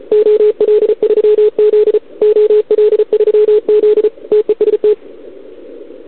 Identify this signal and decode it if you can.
Morse code, heard as a keyed tone, about 34 wpm, 440 Hz. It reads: OL3Z OL3Z TEST
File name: CQWWCW2021 / OL3Z